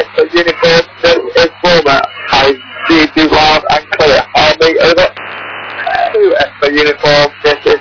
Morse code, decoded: none

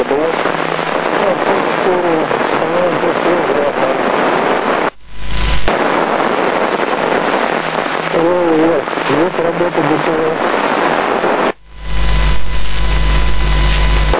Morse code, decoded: none

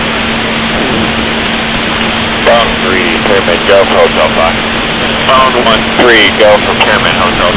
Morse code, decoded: none

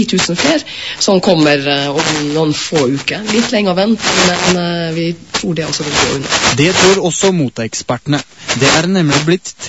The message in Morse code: none

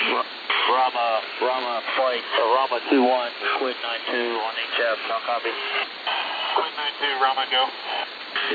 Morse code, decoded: none

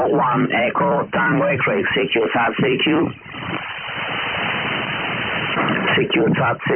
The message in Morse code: none

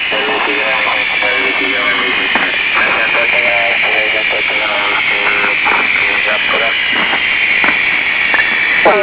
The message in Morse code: none

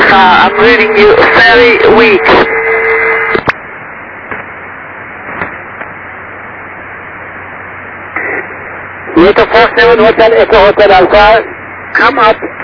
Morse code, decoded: none

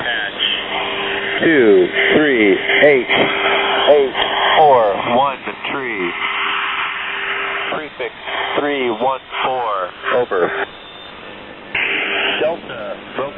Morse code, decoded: none

